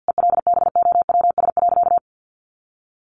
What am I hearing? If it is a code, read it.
Morse code, heard as a keyed tone, about 50 wpm, 700 Hz. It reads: ELBOWS.
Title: A Sound Of Thunder 50wpm